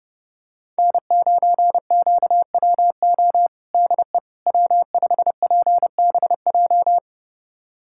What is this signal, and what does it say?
Morse code, heard as a keyed tone, about 30 wpm, 700 Hz. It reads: N9QWO DE W5PBJ
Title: USA Callsign DE Callsign - 5 Characters - 30wpm